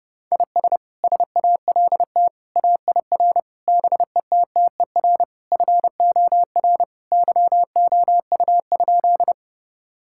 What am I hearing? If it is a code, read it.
Morse code, heard as a keyed tone, about 30 wpm, 700 Hz. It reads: IS SALT AIR BETTER FOR YOU?
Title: Sentences from Top 1000 Words - Review Part 5 - 30wpm